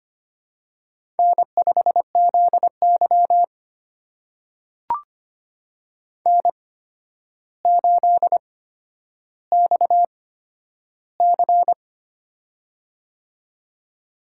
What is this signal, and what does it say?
Morse code, heard as a keyed tone, about 25 wpm, spaced as for 11 wpm, 700 Hz. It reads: N5ZY N8XC
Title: USA Call Signs - 4 Characters - 8x Character Spacing - 25wpm